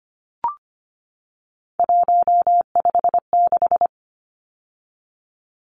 Morse code, heard as a keyed tone, about 25 wpm, 700 Hz. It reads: 156
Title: Numbers - 3-Digit 25wpm